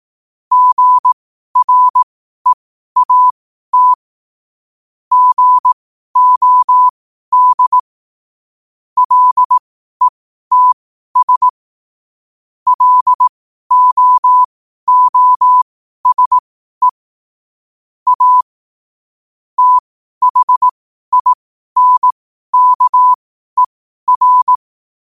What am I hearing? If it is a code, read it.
Morse code, heard as a keyed tone, about 18 wpm, spaced as for 12 wpm, 1000 Hz. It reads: GREAT GOD LETS LOOSE A THINKER
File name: qotdfiles / 12wpm